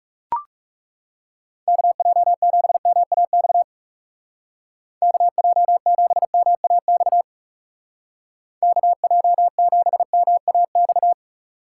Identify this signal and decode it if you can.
Morse code, heard as a keyed tone, about 37 wpm, 700 Hz. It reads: KJ7MAX KJ7MAX KJ7MAX
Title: Speed Racing - USA 2x3 Call Signs 30wpm